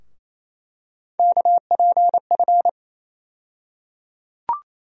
Morse code, heard as a keyed tone, about 27 wpm, 700 Hz. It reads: KPF
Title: USA Call Sign Postfixes - 3 Letters 28wpm